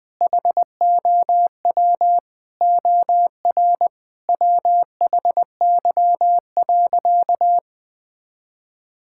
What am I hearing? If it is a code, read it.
Morse code, heard as a keyed tone, about 20 wpm, 700 Hz. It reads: HOW OR WHY.
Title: Sentences from Top 1000 Words - Review Part 6 - 20wpm